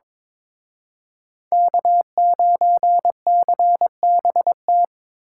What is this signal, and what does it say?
Morse code, heard as a keyed tone, about 25 wpm, 700 Hz. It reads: K9CBT
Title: Speed Racing - USA 1x3 Call Signs 22wpm